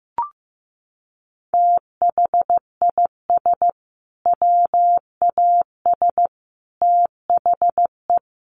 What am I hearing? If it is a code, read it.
Morse code, heard as a keyed tone, about 15 wpm, 700 Hz. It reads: THIS WAS THE
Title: Sentences from Top 1000 Words 15wpm